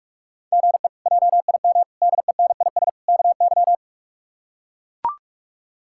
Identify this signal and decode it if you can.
Morse code, heard as a keyed tone, about 42 wpm, 700 Hz. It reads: GE JIM DENIS KY
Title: K1USN SST QSO Practice - 45wpm